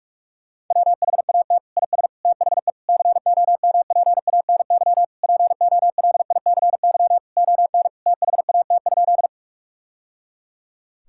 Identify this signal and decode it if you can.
Morse code, heard as a keyed tone, about 46 wpm, 700 Hz. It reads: WHAT IS THE COMPANY POLICY ON THAT?
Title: Sentences from Top 1000 Words - Review Part 5 - 45wpm